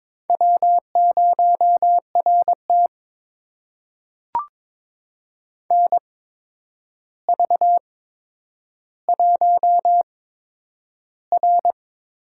Morse code, heard as a keyed tone, about 22 wpm, spaced as for 9 wpm, 700 Hz. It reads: W0RT NV1R
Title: USA Call Signs - 4 Characters - 8x Character Spacing - 22wpm